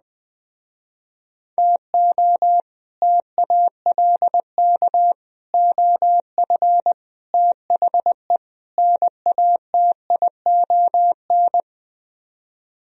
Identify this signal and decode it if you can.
Morse code, heard as a keyed tone, about 20 wpm, 700 Hz. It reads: TO TALK OF THE NATION